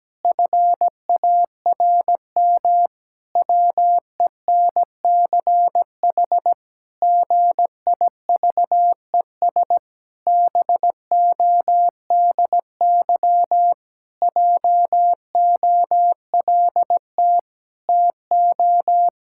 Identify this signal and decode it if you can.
Morse code, heard as a keyed tone, about 17 wpm, 700 Hz. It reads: FARM WENCH GIVES BODY JOLT TO
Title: Pangram Challenge v2 - 17wpm